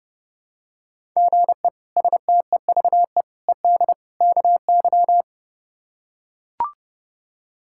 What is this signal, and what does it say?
Morse code, heard as a keyed tone, about 32 wpm, 700 Hz. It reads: GE STEVE ED KY
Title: K1USN SST QSO Practice - 30wpm